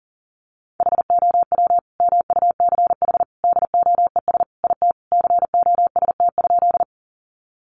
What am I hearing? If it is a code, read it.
Morse code, heard as a keyed tone, about 40 wpm, 700 Hz. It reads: HOW MUCH DOES IT COST?
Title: Sentences from Top 800 Words 40wpm